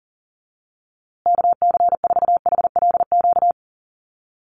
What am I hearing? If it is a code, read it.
Morse code, heard as a keyed tone, about 40 wpm, 700 Hz. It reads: KC4HLQ
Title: USA Call Signs Repeated 3x 40wpm